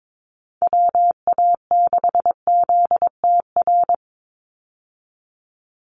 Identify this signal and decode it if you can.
Morse code, heard as a keyed tone, about 22 wpm, 700 Hz. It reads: WA6ZTR
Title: USA Call Signs 22wpm